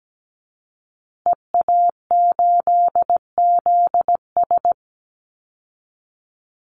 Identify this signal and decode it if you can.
Morse code, heard as a keyed tone, about 17 wpm, 700 Hz. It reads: EA8ZS
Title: Call Signs Repeated 3x - 17wpm